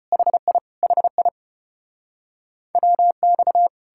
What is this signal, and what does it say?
Morse code, heard as a keyed tone, about 37 wpm, 700 Hz. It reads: HI HI WX